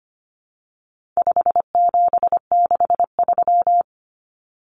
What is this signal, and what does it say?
Morse code, heard as a keyed tone, about 25 wpm, 700 Hz. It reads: 5763
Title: Numbers - 4-Digit 25wpm